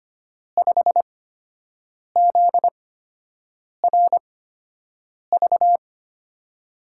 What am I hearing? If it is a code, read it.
Morse code, heard as a keyed tone, about 25 wpm, spaced as for 7 wpm, 700 Hz. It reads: 5ZRV
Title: USA Call Signs - 6 Characters - 8x Character Spacing - 25wpm